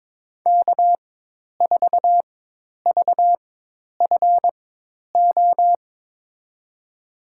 Morse code, heard as a keyed tone, about 23 wpm, spaced as for 10 wpm, 700 Hz. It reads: K4VFO
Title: USA Call Signs - 5 Characters - 4x Character Spacing - 22wpm